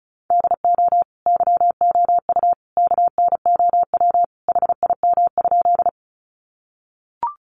Morse code, heard as a keyed tone, about 34 wpm, 700 Hz. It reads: DO YOU KNOW HIM?
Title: Sentences from Top 100 Words 35wpm